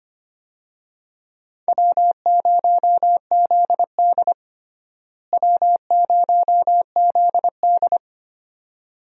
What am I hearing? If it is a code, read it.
Morse code, heard as a keyed tone, about 25 wpm, 700 Hz. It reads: W0ZD W0ZD
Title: USA Call Signs - 4 Characters - Repeated 2x - 25wpm